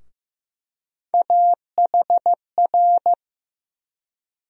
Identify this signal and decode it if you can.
Morse code, heard as a keyed tone, about 15 wpm, 700 Hz. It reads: AHR
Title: USA Call Sign Postfixes - 3 Letters 15wpm